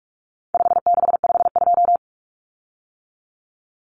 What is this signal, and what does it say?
Morse code, heard as a keyed tone, about 44 wpm, 700 Hz. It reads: 5652
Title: Common Cut Numbers - 4-Digit 45wpm